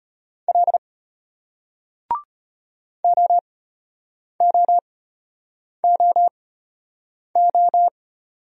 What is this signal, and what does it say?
Morse code, heard as a keyed tone, about 33 wpm, spaced as for 8 wpm, 700 Hz. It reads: L OOOO